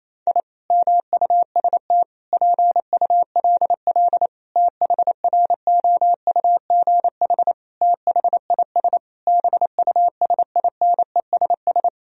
Morse code, heard as a keyed tone, about 28 wpm, 690 Hz. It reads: I MUST PULL THROUGH THIS BUSINESS
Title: Sentences from Top 1000 Words - Review Part 2 - 28wpm